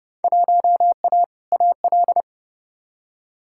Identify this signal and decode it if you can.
Morse code, heard as a keyed tone, about 30 wpm, 700 Hz. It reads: 1A AL
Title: Field Day Exchanges - 30wpm